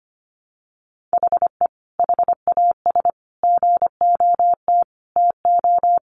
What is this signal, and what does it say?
Morse code, heard as a keyed tone, about 25 wpm, 700 Hz. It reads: HE HAS GOT TO
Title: Sentences from Top 800 Words 25wpm